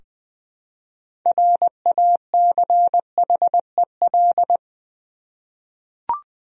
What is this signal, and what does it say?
Morse code, heard as a keyed tone, about 20 wpm, 700 Hz. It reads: RACHEL
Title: USA Common Names - 20wpm